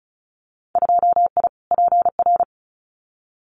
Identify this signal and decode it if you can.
Morse code, heard as a keyed tone, about 34 wpm, 700 Hz. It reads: 2I PR
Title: Winter Field Day Exchanges Encore 35wpm